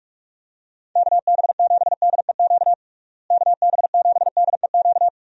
Morse code, heard as a keyed tone, about 46 wpm, 700 Hz. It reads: KB7DEQ KB7DEQ
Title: USA Call Signs - 6 Characters - Repeated 2x - 45wpm